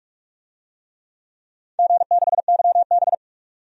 Morse code, heard as a keyed tone, about 44 wpm, 690 Hz. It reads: G6YB
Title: DXCC-100 Call Signs - 4 Characters - 45wpm